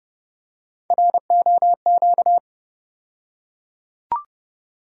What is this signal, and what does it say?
Morse code, heard as a keyed tone, about 29 wpm, 700 Hz. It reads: ROQ
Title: USA Call Sign Postfixes - 3 Letters 30wpm